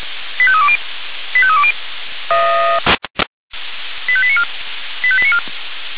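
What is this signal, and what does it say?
Five-tone selective call (ZVEI), heard as quick runs of five tones, tones of about 70 ms. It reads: 85320 85320 85894 85894
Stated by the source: Five-tone ZVEI selective call received on an SDR, from the feuerwehr-orlamuende.org (fire brigade Orlamünde) audio server.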